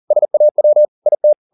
Morse code, heard as a keyed tone, about 40 wpm, 600 Hz. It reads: SAW IT
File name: the old man and the sea 2 40wpm